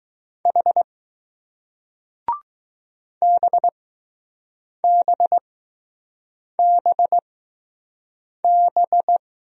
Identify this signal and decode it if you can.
Morse code, heard as a keyed tone, about 20 wpm, spaced as for 6 wpm, 700 Hz. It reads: H BBBB